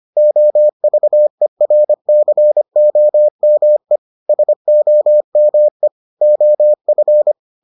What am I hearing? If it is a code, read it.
Morse code, heard as a keyed tone, about 25 wpm, 600 Hz. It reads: OVERCOME SOME OF